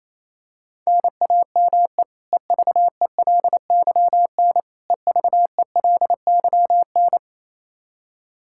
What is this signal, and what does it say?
Morse code, heard as a keyed tone, about 28 wpm, 700 Hz. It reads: NAME EVELYN EVELYN
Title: Practice QSO - Exchange 1 - Beginning - 28wpm